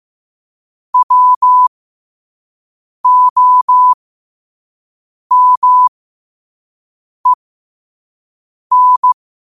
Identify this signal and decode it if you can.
Morse code, heard as a keyed tone, about 15 wpm, spaced as for 5 wpm, 1000 Hz. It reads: WOMEN
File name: 5wpm